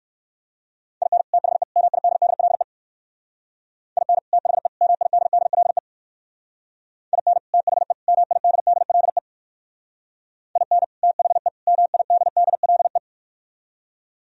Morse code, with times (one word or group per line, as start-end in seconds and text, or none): 1.0-1.2 IN
1.3-1.6 THE
1.8-2.6 MIDDLE
4.0-4.2 IN
4.3-4.7 THE
4.8-5.8 MIDDLE
7.1-7.4 IN
7.5-7.9 THE
8.1-9.2 MIDDLE
10.6-10.8 IN
11.0-11.5 THE
11.7-13.0 MIDDLE